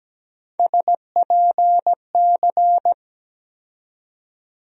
Morse code, heard as a keyed tone, about 17 wpm, 700 Hz. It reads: SPC
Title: USA Call Sign Postfixes - 3 Letters 17wpm